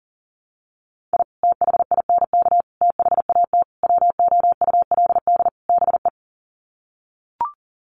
Morse code, heard as a keyed tone, about 40 wpm, 700 Hz. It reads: I THINK THAT WOULD BE
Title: Sets of 5 Words Encore2 40wpm